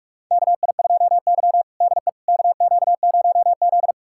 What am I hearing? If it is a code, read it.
Morse code, heard as a keyed tone, about 41 wpm, 700 Hz. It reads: KI2Y DE KQ0Z
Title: USA Callsign DE Callsign - 4 Characters - 45wpm